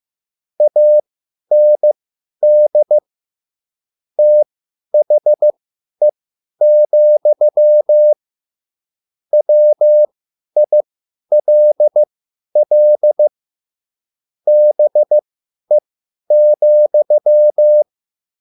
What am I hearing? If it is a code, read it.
Morse code, heard as a keyed tone, about 15 wpm, spaced as for 10 wpm, 600 Hz. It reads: AND THE, WILL BE,
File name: QSL / CW lesson